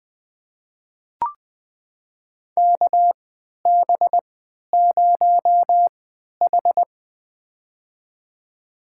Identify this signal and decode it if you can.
Morse code, heard as a keyed tone, about 20 wpm, spaced as for 11 wpm, 700 Hz. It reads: KB0H
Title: USA Call Signs - 4 Characters - 3x Character Spacing - 20wpm